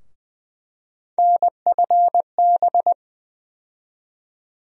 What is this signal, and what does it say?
Morse code, heard as a keyed tone, about 20 wpm, 700 Hz. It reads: NFB
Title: USA Call Sign Postfixes - 3 Letters 20wpm